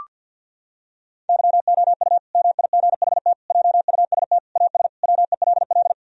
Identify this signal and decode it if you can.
Morse code, heard as a keyed tone, about 50 wpm, 700 Hz. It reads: YOU MIGHT JUST AS WELL
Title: Sentences from Top 1000 Words - Review Part 2 - 50wpm